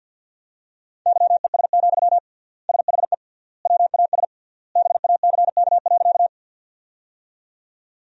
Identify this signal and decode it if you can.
Morse code, heard as a keyed tone, about 49 wpm, 700 Hz. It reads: YES, SHE WAS BACK.
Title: 3x Word-Spacing - Sentences from Top 500 Words - 50wpm